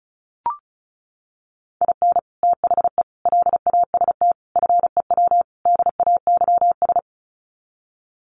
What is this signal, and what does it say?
Morse code, heard as a keyed tone, about 34 wpm, 700 Hz. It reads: IN THE LAST FEW DAYS
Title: Sets of 5 Words Encore2 35wpm